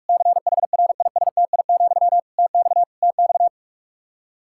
Morse code, heard as a keyed tone, about 46 wpm, 700 Hz. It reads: CHRISTI, TX TX
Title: K1USN SST QSO Practice 45wpm